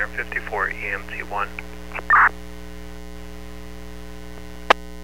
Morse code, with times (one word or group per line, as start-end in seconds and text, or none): none